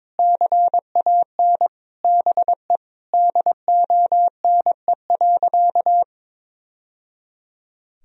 Morse code, heard as a keyed tone, about 22 wpm, 700 Hz. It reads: CAN BE DONE.
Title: Sentences from Top 1000 Words - Review Part 1 - 22wpm